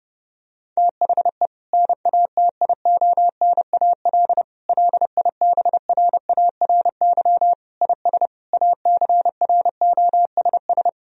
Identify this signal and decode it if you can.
Morse code, heard as a keyed tone, about 32 wpm, 700 Hz. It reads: THE NATIONAL LIBRARY IS ACROSS
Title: Sentences from Top 1000 Words - Review Part 4 - 30wpm